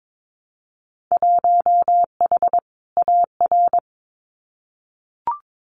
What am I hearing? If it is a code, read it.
Morse code, heard as a keyed tone, about 22 wpm, 700 Hz. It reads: 1H AR
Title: Winter Field Day Exchanges Encore 22wpm